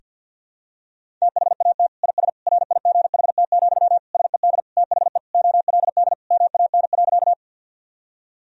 Morse code, heard as a keyed tone, about 51 wpm, 700 Hz. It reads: THAT IS RIGHT, SED THE OLD MAN.
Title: Sentences from Top 200 Words and Abbreviations 50wpm